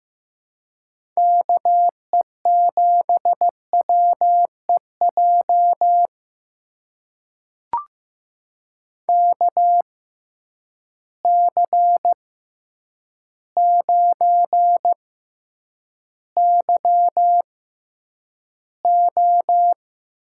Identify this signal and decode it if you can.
Morse code, heard as a keyed tone, about 15 wpm, spaced as for 8 wpm, 700 Hz. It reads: KE7WEJ KC9YO
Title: USA Call Signs - 6 Characters - 6x Character Spacing - 15wpm